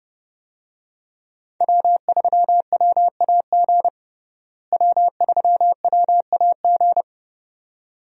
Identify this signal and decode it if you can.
Morse code, heard as a keyed tone, about 31 wpm, 700 Hz. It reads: W3WAG W3WAG
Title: USA Call Signs - 5 Characters - Repeated 2x - 30wpm